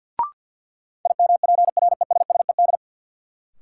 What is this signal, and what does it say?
Morse code, heard as a keyed tone, about 48 wpm, 700 Hz. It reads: IMPRESSED
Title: Sentences from Top 1600 Words - New Word Drill Alt - 50wpm